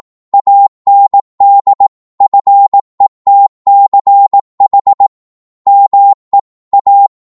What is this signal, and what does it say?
Morse code, heard as a keyed tone, about 18 wpm, 800 Hz. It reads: AND FETCH ME A